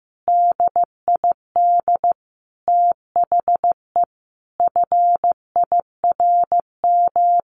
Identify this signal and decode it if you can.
Morse code, heard as a keyed tone, about 15 wpm, 700 Hz. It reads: DID THE FIRM
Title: Sentences from Top 1400 Words 15wpm